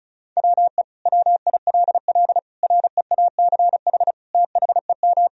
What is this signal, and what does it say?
Morse code, heard as a keyed tone, about 36 wpm, 700 Hz. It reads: WE WILL REACH THEM
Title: Sentences from Top 1000 Words - Review Part 2 - 35wpm